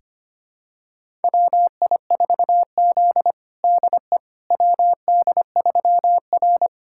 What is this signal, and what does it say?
Morse code, heard as a keyed tone, about 25 wpm, 700 Hz. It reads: WI4Z DE WD3R
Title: USA Callsign DE Callsign - 4 Characters - 25wpm